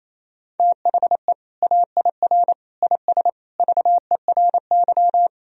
Morse code, heard as a keyed tone, about 28 wpm, 700 Hz. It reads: THE AIR IS VERY
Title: Sentences from Top 1000 Words - Review Part 1 - 28wpm